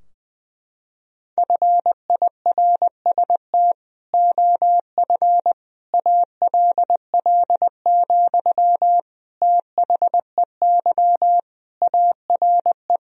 Morse code, heard as a keyed tone, about 21 wpm, 700 Hz. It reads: FIRST OF ALL, THEY ARE
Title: Sentences from Top 1000 Words - Review Part 4 - 20wpm